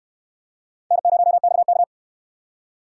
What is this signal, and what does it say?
Morse code, heard as a keyed tone, about 63 wpm, 700 Hz. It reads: N0XK